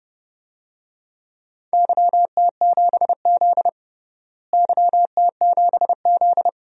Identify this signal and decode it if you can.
Morse code, heard as a keyed tone, about 32 wpm, 700 Hz. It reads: YT7Z YT7Z
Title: DXCC-100 Call Signs - 4 Characters - Repeated 2x - 30wpm